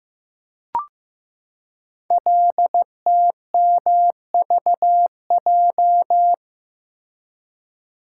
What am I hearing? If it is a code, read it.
Morse code, heard as a keyed tone, about 15 wpm, 700 Hz. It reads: LTMVJ
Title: Code Groups - 5 Letters - 15wpm